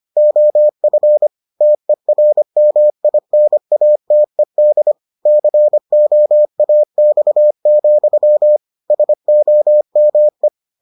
Morse code, heard as a keyed tone, about 25 wpm, 600 Hz. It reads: OF TERMINATED COAX, SOME